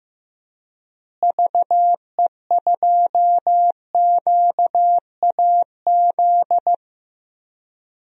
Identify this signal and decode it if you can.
Morse code, heard as a keyed tone, about 15 wpm, 700 Hz. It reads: VE2QAZ
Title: DXCC-100 Call Signs - 6 Characters - Repeated 2x - 15wpm